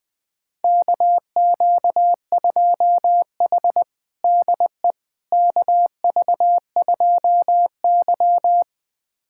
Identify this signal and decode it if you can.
Morse code, heard as a keyed tone, about 20 wpm, 700 Hz. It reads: KQ2H DE KV2Y
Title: USA Callsign DE Callsign - 4 Characters - 20wpm